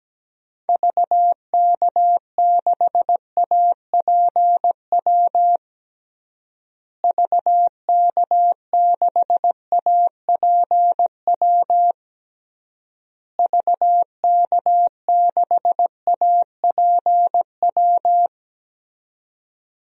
Morse code, heard as a keyed tone, about 17 wpm, 700 Hz. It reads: VK6APW VK6APW VK6APW
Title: DXCC-100 Call Signs - 6 Characters - Repeated 3x - 17wpm